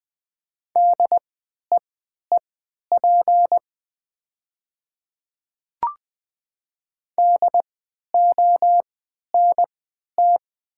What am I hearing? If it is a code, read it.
Morse code, heard as a keyed tone, about 20 wpm, spaced as for 11 wpm, 700 Hz. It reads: DEEP DONT
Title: 3x Character Spacing - Top 500 Words - 20wpm